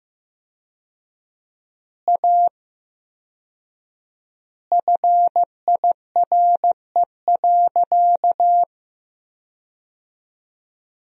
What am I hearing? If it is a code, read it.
Morse code, heard as a keyed tone, about 15 wpm, 700 Hz. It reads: A FIRE.
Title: 4x Word-Spacing - Sentences from Top 400 Words - 15wpm